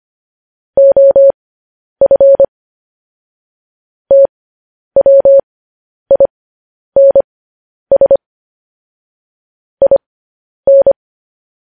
Morse code, heard as a keyed tone, about 25 wpm, spaced as for 10 wpm, 550 Hz. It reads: OF TWINS IN
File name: podcast / News headlines-1x-25x10